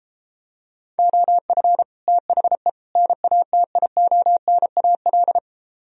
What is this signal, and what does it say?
Morse code, heard as a keyed tone, about 33 wpm, 700 Hz. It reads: OF THE NATIONAL